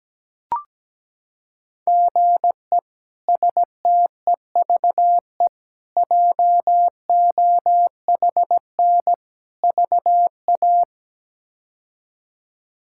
Morse code, heard as a keyed tone, about 17 wpm, 700 Hz. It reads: GE STEVE JOHN VA
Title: K1USN SST QSO Practice - 17wpm